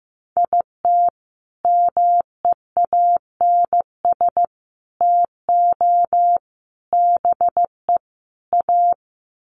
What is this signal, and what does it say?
Morse code, heard as a keyed tone, about 15 wpm, 700 Hz. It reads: IT MEANS TO BE A
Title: Sets of 5 Words Encore2 15wpm